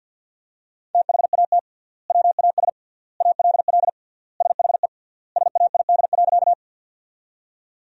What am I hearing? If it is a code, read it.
Morse code, heard as a keyed tone, about 50 wpm, 700 Hz. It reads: THAT WAS ALL SHE SAID.
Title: 3x Word-Spacing - Sentences from Top 100 Words - 50wpm